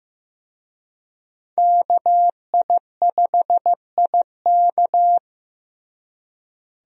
Morse code, heard as a keyed tone, about 15 wpm, 700 Hz. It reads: KI5IK
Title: USA Call Signs - 5 Characters - Repeated 2x - 15wpm